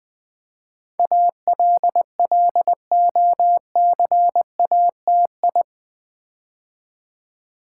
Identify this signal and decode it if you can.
Morse code, heard as a keyed tone, about 20 wpm, 700 Hz. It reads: ALLOCATI
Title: Build A Word - 4 Letters At A Time - 10-Letter Words - 20wpm